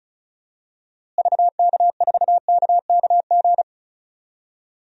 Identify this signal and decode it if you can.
Morse code, heard as a keyed tone, about 34 wpm, 700 Hz. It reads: VK4KKG
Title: DXCC-100 Call Signs - 6 Characters - Repeated 2x - 35wpm